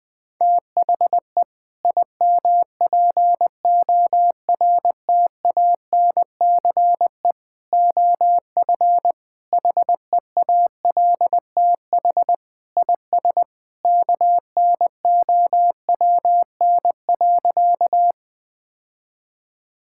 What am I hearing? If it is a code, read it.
Morse code, heard as a keyed tone, about 20 wpm, 700 Hz. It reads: THE IMPORTANCE OF HEALTH IS KNOWN.
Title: Sentences with 3-Word Phrases - 20wpm